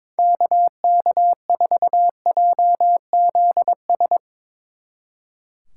Morse code, heard as a keyed tone, about 24 wpm, 700 Hz. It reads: KK4JZS
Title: Speed Racing - USA 2x3 Call Signs 22wpm